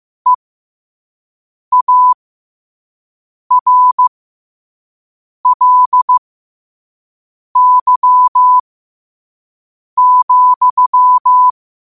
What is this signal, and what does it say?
Morse code, heard as a keyed tone, about 15 wpm, spaced as for 5 wpm, 1000 Hz. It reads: EARLY,
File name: qotdfiles / 5wpm